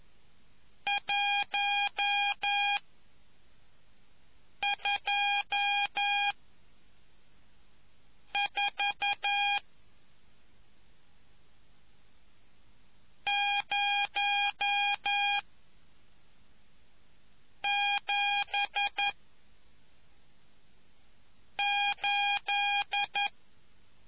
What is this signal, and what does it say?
Morse code, heard as a keyed tone, about 11 wpm, 2450 Hz. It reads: 1 2 4 0 7 8